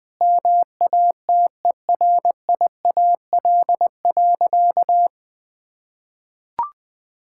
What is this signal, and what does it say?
Morse code, heard as a keyed tone, about 19 wpm, 700 Hz. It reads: MATERIAL.
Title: Sentences from Top 1000 Words - Review Part 5 - 20wpm